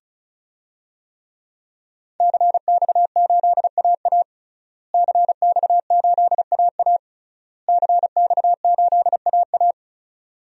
Morse code, heard as a keyed tone, about 35 wpm, 700 Hz. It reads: CX8AA CX8AA CX8AA
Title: DXCC-100 Call Signs - 5 Characters - Repeated 3x - 35wpm